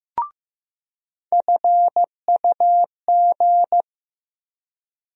Morse code, heard as a keyed tone, about 15 wpm, 700 Hz. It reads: FUG
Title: USA Call Sign Postfixes - 3 Letters 15wpm